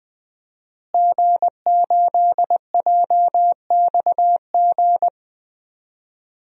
Morse code, heard as a keyed tone, about 20 wpm, 700 Hz. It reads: G8JXG
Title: DXCC-100 Call Signs - 5 Characters - Repeated 2x - 20wpm